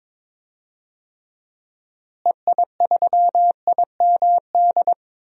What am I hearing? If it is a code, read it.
Morse code, heard as a keyed tone, about 22 wpm, 700 Hz. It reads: EI3IMD
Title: DXCC-100 Call Signs - 6 Characters - Repeated 2x - 22wpm